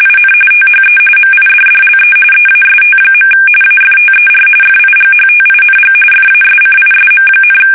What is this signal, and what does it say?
RTTY, about 45.45 baud. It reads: IN THE MIDDLE EAST. AFTER FIGHTING FASCISM,
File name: fsk850 / battleshipnj